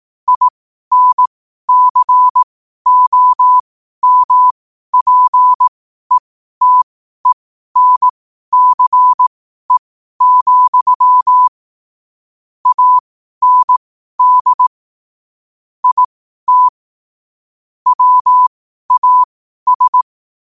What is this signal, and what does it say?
Morse code, heard as a keyed tone, about 18 wpm, spaced as for 12 wpm, 1000 Hz. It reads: INCOMPETENCE, AND IT WAS
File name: qotdfiles / 12wpm